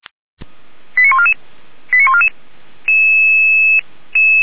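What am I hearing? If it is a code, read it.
Five-tone selective call (ZVEI), heard as quick runs of five tones, tones of about 70 ms. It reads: 89144 89144
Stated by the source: ZVEI five-tone call received on SDR.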